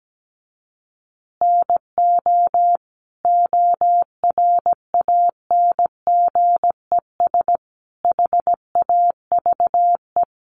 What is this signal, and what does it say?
Morse code, heard as a keyed tone, about 17 wpm, 700 Hz. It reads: NO ORANGES HAVE
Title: Easy Phrases 17wpm